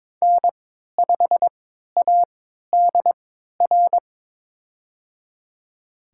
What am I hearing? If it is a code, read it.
Morse code, heard as a keyed tone, about 22 wpm, spaced as for 12 wpm, 700 Hz. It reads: N5ADR
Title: USA Call Signs - 5 Characters - 3x Character Spacing - 22wpm